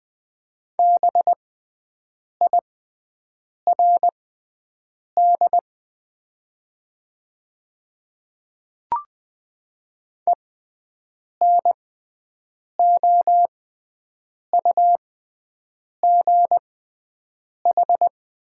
Morse code, heard as a keyed tone, about 20 wpm, spaced as for 7 wpm, 700 Hz. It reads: BIRD ENOUGH
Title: 6x Character Spacing - Top 500 Words - 20wpm